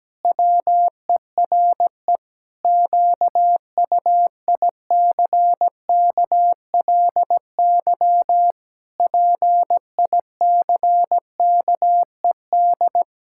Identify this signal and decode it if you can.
Morse code, heard as a keyed tone, about 17 wpm, 700 Hz. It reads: WERE QUICKLY PICKED
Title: Pangram Challenge v2 - 17wpm